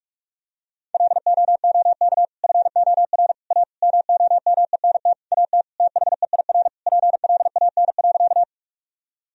Morse code, heard as a keyed tone, about 46 wpm, 700 Hz. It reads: LOOK FOR A MOMENT AT THEIR PLAN.
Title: Sentences from Top 1000 Words - Review Part 2 - 45wpm